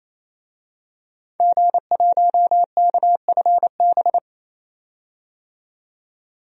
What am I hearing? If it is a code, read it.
Morse code, heard as a keyed tone, about 28 wpm, 700 Hz. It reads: G1KFB
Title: DXCC-100 Call Signs - 5 Characters - 28wpm